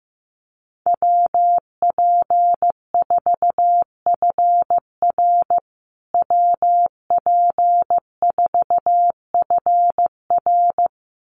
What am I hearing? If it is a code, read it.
Morse code, heard as a keyed tone, about 15 wpm, 700 Hz. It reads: WP4FR WP4FR
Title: USA Call Signs Repeated 2x 15wpm